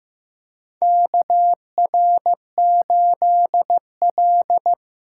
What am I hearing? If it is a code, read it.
Morse code, heard as a keyed tone, about 15 wpm, 700 Hz. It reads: KR8L
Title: K1USN SST QSO Practice - 15wpm